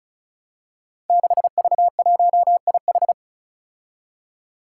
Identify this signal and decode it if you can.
Morse code, heard as a keyed tone, about 36 wpm, 700 Hz. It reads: 6V1IH
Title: Five Challenging Letter-Number Combinations 35wpm